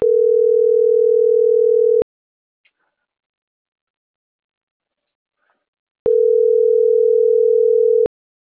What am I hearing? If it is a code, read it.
Ringback tone (North America), 440 + 480 Hz.
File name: CENTRAL NMP / out marisol.monzalvonmp